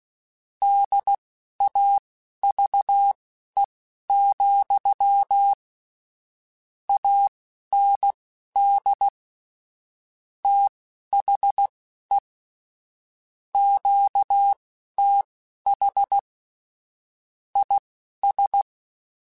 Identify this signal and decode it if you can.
Morse code, heard as a keyed tone, about 16 wpm, spaced as for 11 wpm, 800 Hz. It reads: DAVE, AND THE QTH IS
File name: QSOs / QSOs-mp3s / WA7QOX 2@10